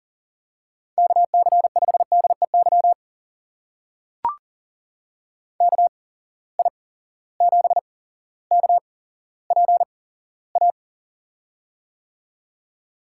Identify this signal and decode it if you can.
Morse code, heard as a keyed tone, about 39 wpm, spaced as for 17 wpm, 700 Hz. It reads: KC5DEY KI7KPA
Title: USA Call Signs - 6 Characters - 8x Character Spacing - 40wpm